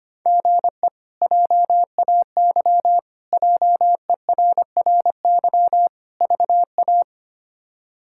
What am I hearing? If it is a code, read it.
Morse code, heard as a keyed tone, about 25 wpm, 700 Hz. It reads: GE JAY JERRY VA
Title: K1USN SST QSO Practice - 25wpm